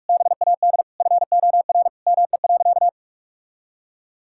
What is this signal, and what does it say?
Morse code, heard as a keyed tone, about 41 wpm, 700 Hz. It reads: BAD FOR ME.
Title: Sentences from Top 1600 Words 45wpm